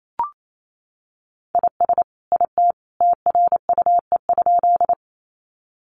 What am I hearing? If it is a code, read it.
Morse code, heard as a keyed tone, about 28 wpm, 700 Hz. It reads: IS IT TRUE?
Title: Sentences from Top 400 Words 28wpm